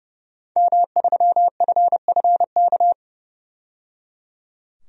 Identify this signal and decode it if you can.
Morse code, heard as a keyed tone, about 29 wpm, 700 Hz. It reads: M3FFK
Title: DXCC-100 Call Signs - 5 Characters - 30wpm